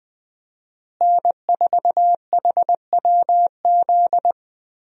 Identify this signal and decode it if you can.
Morse code, heard as a keyed tone, about 20 wpm, 700 Hz. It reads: N4HWZ
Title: USA Call Signs - 5 Characters - Repeated 2x - 20wpm